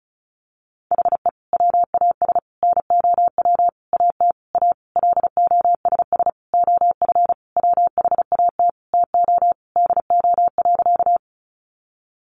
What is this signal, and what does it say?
Morse code, heard as a keyed tone, about 34 wpm, 700 Hz. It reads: HE WAS NOW AT A LOSS OF WHAT TO DO.
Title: Sentences from Top 1400 Words 35wpm